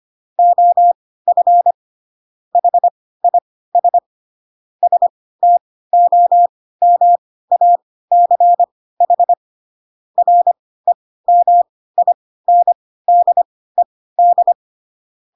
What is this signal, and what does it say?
Morse code, heard as a keyed tone, about 25 wpm, spaced as for 16 wpm, 700 Hz. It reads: OF HIS STOMACH REMINDED